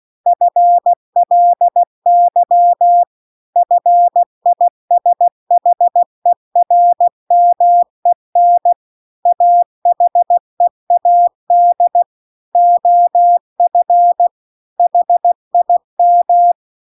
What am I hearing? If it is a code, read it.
Morse code, heard as a keyed tone, about 16 wpm, 700 Hz. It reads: FLY FISHERMEN AHEAD OF HIM
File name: CW-The-Big-Twoheaded-River-2-16WPM